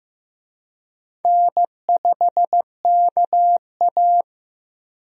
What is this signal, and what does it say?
Morse code, heard as a keyed tone, about 15 wpm, 700 Hz. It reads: N5KA
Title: USA Call Signs - 4 Characters - Repeated 2x - 15wpm